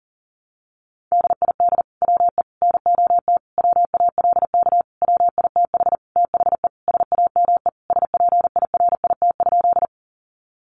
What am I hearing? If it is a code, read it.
Morse code, heard as a keyed tone, about 40 wpm, 700 Hz. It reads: DID WE NOT WALK WITH THE SAME SPIRIT?
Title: Sentences from Top 1400 Words 40wpm